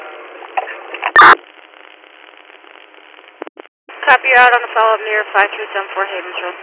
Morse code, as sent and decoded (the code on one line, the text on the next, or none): none